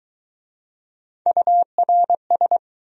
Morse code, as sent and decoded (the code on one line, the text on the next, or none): ..- .-. ...
URS